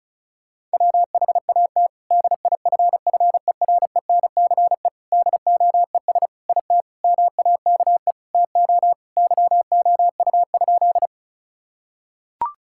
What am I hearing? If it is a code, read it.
Morse code, heard as a keyed tone, about 34 wpm, 700 Hz. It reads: WHAT DIFFERENCE DOES IT MAKE TO YOU?
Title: Sentences from Top 1000 Words - Review Part 5 - 35wpm